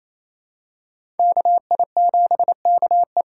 Morse code, heard as a keyed tone, about 28 wpm, 700 Hz. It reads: KI7KE
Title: USA Call Signs - 6 Characters - 6x Character Spacing - 28wpm